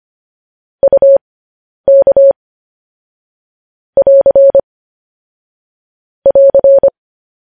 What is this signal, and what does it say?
Morse code, heard as a keyed tone, about 25 wpm, spaced as for 10 wpm, 550 Hz. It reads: UK <AR> <AR>
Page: News Headlines in Morse Code at 10 WPM